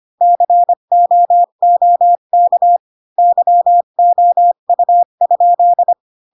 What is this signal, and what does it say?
Morse code, heard as a keyed tone, about 25 wpm, 700 Hz. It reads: COOK YOU?
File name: Pinocchio / Pinocchio ch5 20 25 ik1hge